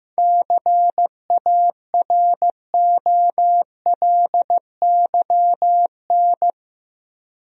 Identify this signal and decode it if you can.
Morse code, heard as a keyed tone, about 15 wpm, 700 Hz. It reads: CAROLYN
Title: USA Common Names - 15wpm